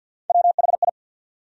Morse code, heard as a keyed tone, about 53 wpm, 700 Hz. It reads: WHI
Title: USA Call Sign Postfixes - 3 Letters 50wpm